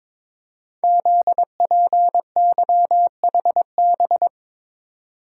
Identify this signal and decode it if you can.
Morse code, heard as a keyed tone, about 22 wpm, 700 Hz. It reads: ZPYHB
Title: Code Groups - 5 Letter:Numbers - 22wpm